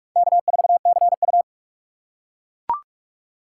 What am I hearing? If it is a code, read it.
Morse code, heard as a keyed tone, about 41 wpm, 700 Hz. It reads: K4CU
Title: USA Call Signs - 4 Characters - 3x Character Spacing - 45wpm